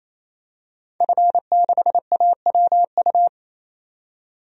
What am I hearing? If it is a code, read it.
Morse code, heard as a keyed tone, about 28 wpm, 700 Hz. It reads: F6AWU
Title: DXCC-100 Call Signs - 5 Characters - 28wpm